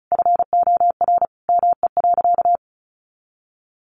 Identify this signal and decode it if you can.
Morse code, heard as a keyed tone, about 34 wpm, 700 Hz. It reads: FOR ME.
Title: Sentences from Top 300 Words 35wpm